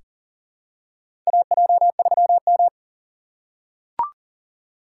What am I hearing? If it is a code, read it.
Morse code, heard as a keyed tone, about 37 wpm, 700 Hz. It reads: AJ3M
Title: USA Call Signs - 4 Characters - 3x Character Spacing - 40wpm